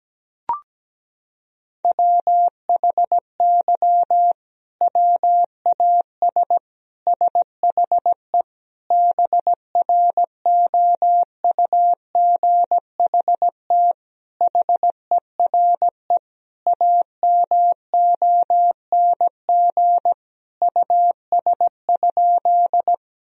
Morse code, heard as a keyed tone, about 17 wpm, 700 Hz. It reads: WHY WAS SHE BROUGHT HERE AMONG US?
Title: Sentences from Top 1000 Words - Review Part 3 - 17wpm